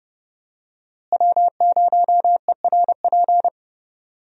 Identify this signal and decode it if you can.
Morse code, heard as a keyed tone, about 32 wpm, 700 Hz. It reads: W0ERP
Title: USA Call Signs - 5 Characters - Repeated 2x - 30wpm